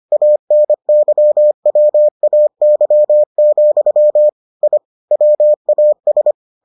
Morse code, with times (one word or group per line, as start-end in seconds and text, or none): 0.1-4.3 ANYWAY,
4.6-4.8 I
5.1-6.3 WAS